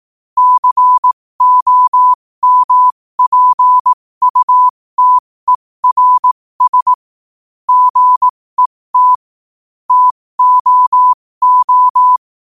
Morse code, heard as a keyed tone, about 18 wpm, spaced as for 15 wpm, 1000 Hz. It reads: COMPUTERS GET TOO
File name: qotdfiles / 15wpm